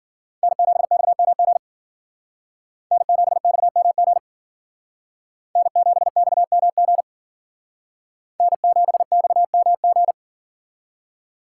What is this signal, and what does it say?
Morse code, heard as a keyed tone, about 49 wpm, 700 Hz. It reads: N7XMG N7XMG N7XMG N7XMG
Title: Speed Racing - USA 1x3 Call Signs 40wpm